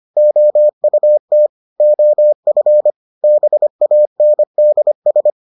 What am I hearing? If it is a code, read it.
Morse code, heard as a keyed tone, about 25 wpm, 600 Hz. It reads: OUT OF BANDS